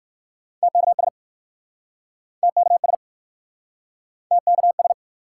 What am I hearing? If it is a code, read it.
Morse code, heard as a keyed tone, about 60 wpm, 700 Hz. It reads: TKS TKS TKS